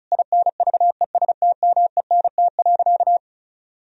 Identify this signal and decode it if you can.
Morse code, heard as a keyed tone, about 34 wpm, 700 Hz. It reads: INVESTMENT.
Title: Sentences from Top 1000 Words - Review Part 4 - 35wpm